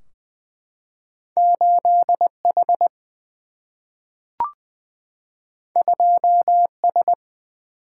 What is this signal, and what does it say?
Morse code, heard as a keyed tone, about 20 wpm, 700 Hz. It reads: 8H 2S
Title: Single-Letter Single-Number 20wpm